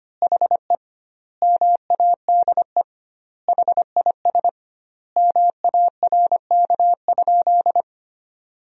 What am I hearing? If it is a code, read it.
Morse code, heard as a keyed tone, about 25 wpm, 700 Hz. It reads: HE MADE HIS MARK?
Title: 2x Word-Spacing - Sentences from Top 300 Words - 25wpm